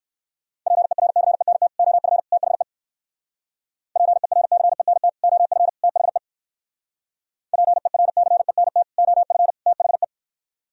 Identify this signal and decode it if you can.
Morse code, heard as a keyed tone, about 61 wpm, 700 Hz. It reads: PERCENT OF THE PERCENT OF THE PERCENT OF THE